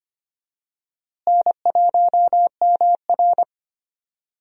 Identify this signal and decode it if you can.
Morse code, heard as a keyed tone, about 25 wpm, 700 Hz. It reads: N1MR
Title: USA Call Signs - 4 Characters - Repeated 2x - 25wpm